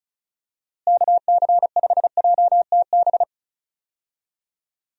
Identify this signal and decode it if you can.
Morse code, heard as a keyed tone, about 34 wpm, 700 Hz. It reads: KC5JTB
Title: USA Call Signs - 6 Characters - Repeated 3x - 35wpm